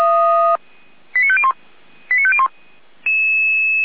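Five-tone selective call (ZVEI), heard as quick runs of five tones, tones of about 70 ms. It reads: 89501 89501
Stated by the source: ZVEI five-tone selective call received by SDR.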